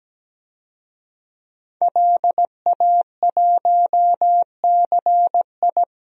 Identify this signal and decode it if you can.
Morse code, heard as a keyed tone, about 17 wpm, 700 Hz. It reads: LA1CI
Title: DXCC-100 Call Signs - 5 Characters - 17wpm